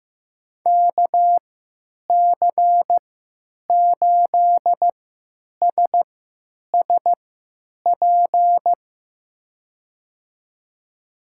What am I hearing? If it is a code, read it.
Morse code, heard as a keyed tone, about 15 wpm, spaced as for 9 wpm, 700 Hz. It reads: KC8SSP
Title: USA Call Signs - 6 Characters - 3x Character Spacing - 15wpm